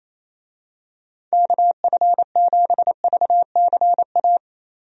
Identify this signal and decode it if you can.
Morse code, heard as a keyed tone, about 27 wpm, 700 Hz. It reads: KF7VCA
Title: USA Call Signs - 6 Characters - 28wpm